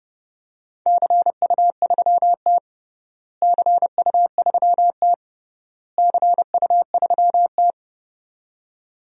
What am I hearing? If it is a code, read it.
Morse code, heard as a keyed tone, about 32 wpm, 700 Hz. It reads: CU3T CU3T CU3T
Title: DXCC-100 Call Signs - 4 Characters - Repeated 3x - 30wpm